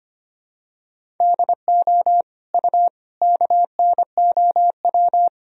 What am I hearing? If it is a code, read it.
Morse code, heard as a keyed tone, about 25 wpm, 700 Hz. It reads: DO U KNOW
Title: Sentences from Top 300 Words and Abbreviations 25wpm